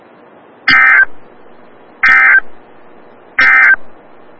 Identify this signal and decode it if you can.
Emergency Alert System (SAME) tones, heard as harsh buzzing bursts. It is heard from 0.6 to 3.8 s.
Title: Des Moines, IA NOAA Weather Radio - WXL57